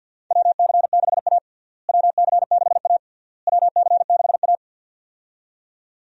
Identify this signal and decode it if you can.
Morse code, heard as a keyed tone, about 53 wpm, 700 Hz. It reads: WC6A WC6A WC6A
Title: USA Call Signs - 4 Characters - Repeated 3x - 50wpm